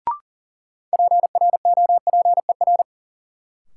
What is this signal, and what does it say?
Morse code, heard as a keyed tone, about 40 wpm, 700 Hz. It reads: PROPER
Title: Sentences from Top 800 Words - New Word Drill - 40wpm